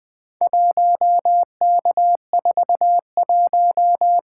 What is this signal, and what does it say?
Morse code, heard as a keyed tone, about 20 wpm, 700 Hz. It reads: 1K41
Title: Four Letter-Number Combinations - 20wpm